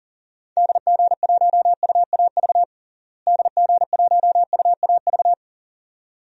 Morse code, heard as a keyed tone, about 40 wpm, 700 Hz. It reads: DG1UAV DG1UAV
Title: DXCC-100 Call Signs - 6 Characters - Repeated 2x - 40wpm